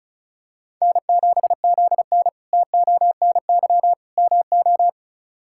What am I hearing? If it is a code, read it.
Morse code, heard as a keyed tone, about 36 wpm, 700 Hz. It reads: N7ZN TONY MO
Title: K1USN SST QSO Practice - 35wpm